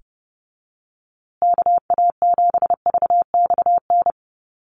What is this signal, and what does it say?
Morse code, heard as a keyed tone, about 29 wpm, 700 Hz. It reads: KA7VXN
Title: USA Call Signs Repeated 3x 30wpm